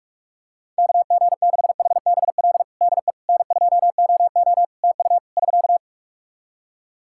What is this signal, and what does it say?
Morse code, heard as a keyed tone, about 46 wpm, 700 Hz. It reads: KG6HBL DE N2OO TU <SK>
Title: K1USN SST QSO Practice 45wpm